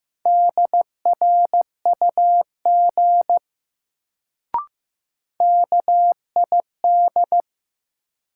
Morse code, heard as a keyed tone, about 15 wpm, 700 Hz. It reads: DRUG KID
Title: Sets of 5 Words - New Word Drill - 15wpm